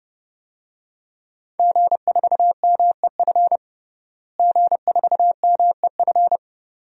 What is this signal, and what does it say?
Morse code, heard as a keyed tone, about 30 wpm, 700 Hz. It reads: G4MEF G4MEF
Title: DXCC-100 Call Signs - 5 Characters - Repeated 2x - 30wpm